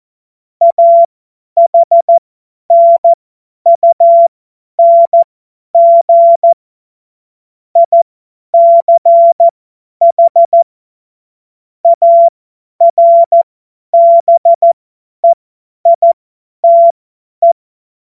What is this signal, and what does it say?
Morse code, heard as a keyed tone, about 14 wpm, spaced as for 10 wpm, 680 Hz. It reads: AHNUNG ICH ARBEITE
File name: Morsekurs / Witze / Witz 25. Mai 2021, 50 F70